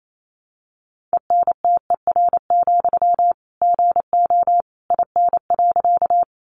Morse code, heard as a keyed tone, about 28 wpm, 700 Hz. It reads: ENTER, GO IN.
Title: Sentences from Top 1300 Words 28wpm